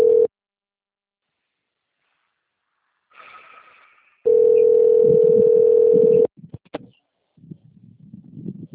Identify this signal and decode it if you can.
Ringback tone (North America), 440 + 480 Hz.